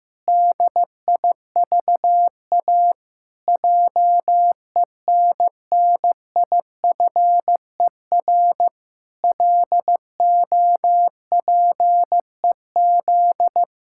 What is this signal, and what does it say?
Morse code, heard as a keyed tone, about 15 wpm, 700 Hz. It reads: DIVA JENNIFER LOPEZ
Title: Pangram Challenge v2 - 15wpm